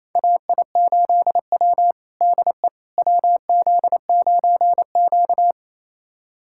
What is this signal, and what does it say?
Morse code, heard as a keyed tone, about 28 wpm, 700 Hz. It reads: AI8W DE WZ9Q
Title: USA Callsign DE Callsign - 4 Characters - 28wpm